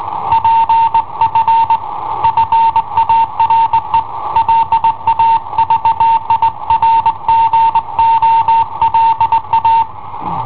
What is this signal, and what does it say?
Morse code, heard as a keyed tone, about 18 wpm, 910 Hz. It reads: PF FARE LA VIRGOLA
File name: i7ohp / cassetta i7ohp B